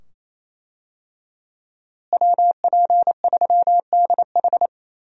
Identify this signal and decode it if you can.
Morse code, heard as a keyed tone, about 28 wpm, 700 Hz. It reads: WP3DH
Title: USA Call Signs - 5 Characters - Repeated 2x - 28wpm